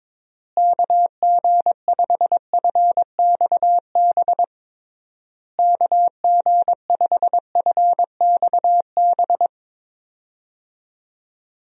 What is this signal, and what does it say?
Morse code, heard as a keyed tone, about 22 wpm, 700 Hz. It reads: KG5FXB KG5FXB
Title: USA Call Signs - 6 Characters - Repeated 3x - 22wpm